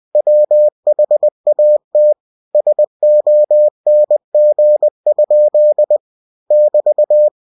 Morse code, heard as a keyed tone, about 20 wpm, 600 Hz. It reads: WHAT SONG? <BT>
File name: winnie-20wpm